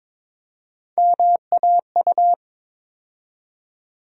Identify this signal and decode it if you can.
Morse code, heard as a keyed tone, about 22 wpm, 700 Hz. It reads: MAU
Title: USA Call Sign Postfixes - 3 Letters 22wpm